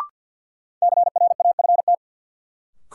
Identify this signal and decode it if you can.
Morse code, heard as a keyed tone, about 48 wpm, 700 Hz. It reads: CRAFT
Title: Sentences from Top 1000 Words - Review Part 6 - New Word Drill - 50wpm